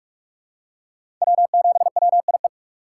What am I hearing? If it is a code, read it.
Morse code, heard as a keyed tone, about 44 wpm, 700 Hz. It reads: W7WIE
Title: USA Call Signs - 5 Characters - 6x Character Spacing - 45wpm